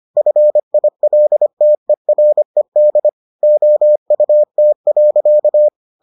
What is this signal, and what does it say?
Morse code, heard as a keyed tone, about 25 wpm, 600 Hz. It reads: FILTERED OUT.